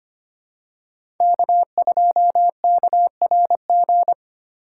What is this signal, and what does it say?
Morse code, heard as a keyed tone, about 25 wpm, 700 Hz. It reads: K2KRG
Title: USA Call Signs - 5 Characters - 8x Character Spacing - 25wpm